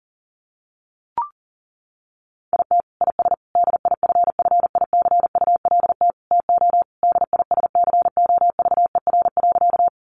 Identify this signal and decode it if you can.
Morse code, heard as a keyed tone, about 40 wpm, 700 Hz. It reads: IT IS DIFFICULT TO DISCOVER.
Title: Sentences from Top 1100 Words 40wpm